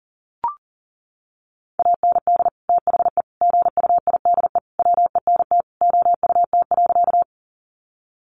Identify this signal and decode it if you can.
Morse code, heard as a keyed tone, about 40 wpm, 700 Hz. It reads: AND THE GUIDE WENT OUT.
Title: Sentences from Top 600 Words 40wpm